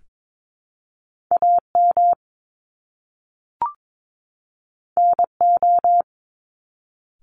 Morse code, heard as a keyed tone, about 22 wpm, 700 Hz. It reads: AM NO
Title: Top Two Letter Combinations in Words 22wpm